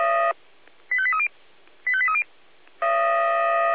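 Five-tone selective call (ZVEI), heard as quick runs of five tones, tones of about 70 ms. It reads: 85830 85830
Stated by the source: ZVEI five-tone selective call 85882 received on SDR.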